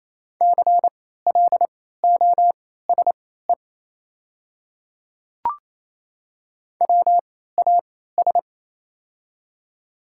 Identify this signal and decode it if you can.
Morse code, heard as a keyed tone, about 28 wpm, spaced as for 16 wpm, 700 Hz. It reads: CLOSE WAS
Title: 3x Character Spacing - Top 300 Words - 28wpm